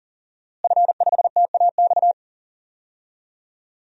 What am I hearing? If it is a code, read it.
Morse code, heard as a keyed tone, about 40 wpm, 690 Hz. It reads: F5TAX
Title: DXCC-100 Call Signs - 5 Characters - Repeated 2x - 40wpm